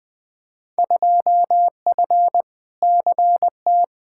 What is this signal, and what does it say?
Morse code, heard as a keyed tone, about 20 wpm, 700 Hz. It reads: 2F CT
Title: Field Day Exchanges Encore - 20wpm